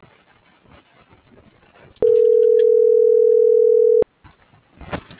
Ringback tone (North America), 440 + 480 Hz.